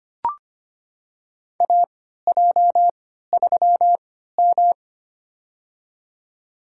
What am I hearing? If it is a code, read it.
Morse code, heard as a keyed tone, about 25 wpm, spaced as for 14 wpm, 700 Hz. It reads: AJ3M
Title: USA Call Signs - 3x Character Spacing - 25wpm